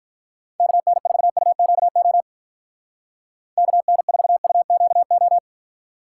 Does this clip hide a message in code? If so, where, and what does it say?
Yes: Morse code from 0.6 to 5.4 s, heard as a keyed tone, about 44 wpm, 700 Hz. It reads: KN4UQO KN4UQO